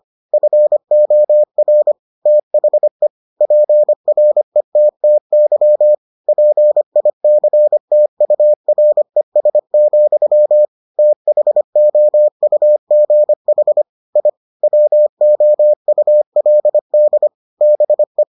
Morse code, heard as a keyed tone, about 25 wpm, 600 Hz. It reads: FOR THE PRETTY PICTURES, THOUGH I WOULD BE